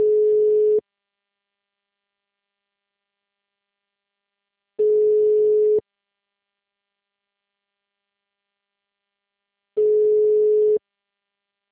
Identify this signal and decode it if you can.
Ringback tone (Europe, 425 Hz).
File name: out- -2030- - - .12